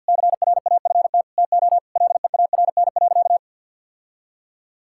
Morse code, heard as a keyed tone, about 50 wpm, 700 Hz. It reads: CRAFT TO LEARN.
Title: Sentences from Top 1000 Words - Review Part 6 - 50wpm